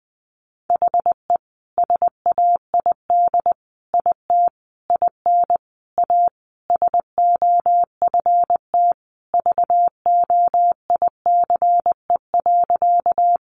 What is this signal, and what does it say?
Morse code, heard as a keyed tone, about 20 wpm, 700 Hz. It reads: HE SAID IT IN A SOFT VOICE.
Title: Sentences from Top 1500 Words 20wpm